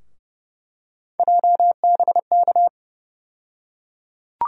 Morse code, heard as a keyed tone, about 30 wpm, 700 Hz. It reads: JBK
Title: USA Call Sign Postfixes - 3 Letters 30wpm